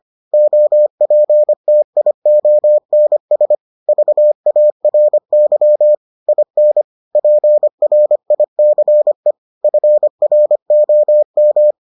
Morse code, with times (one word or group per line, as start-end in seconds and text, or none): 0.3-3.5 OPTIONS
3.9-5.9 VARY
6.3-6.8 IN
7.2-9.3 PRICE
9.6-11.7 FROM